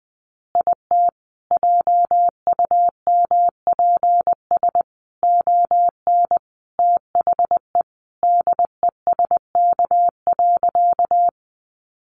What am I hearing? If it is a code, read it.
Morse code, heard as a keyed tone, about 20 wpm, 700 Hz. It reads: IT JUMPS ON THE DESK.
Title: Additional Easy Phrases 20wpm